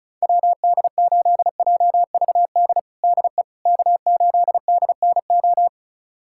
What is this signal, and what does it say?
Morse code, heard as a keyed tone, about 34 wpm, 700 Hz. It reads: WD8JVD DE K8DNO
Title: USA Callsign DE Callsign 35wpm